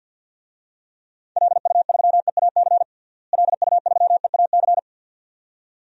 Morse code, heard as a keyed tone, about 52 wpm, 700 Hz. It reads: LU3EAC LU3EAC
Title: DXCC-100 Call Signs - 6 Characters - Repeated 2x - 50wpm